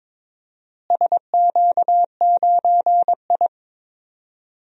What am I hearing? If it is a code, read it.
Morse code, heard as a keyed tone, about 22 wpm, 700 Hz. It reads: SQ9I
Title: DXCC-100 Call Signs - 4 Characters - Repeated 2x - 22wpm